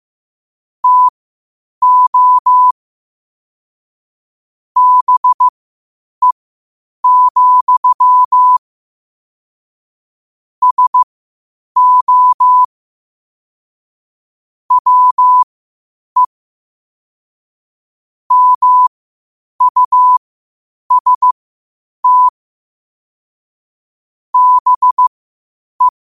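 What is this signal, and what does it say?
Morse code, heard as a keyed tone, about 15 wpm, spaced as for 8 wpm, 1000 Hz. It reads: TO BE, SO WE MUST BE
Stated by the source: Morse code quotes at 8 WPM.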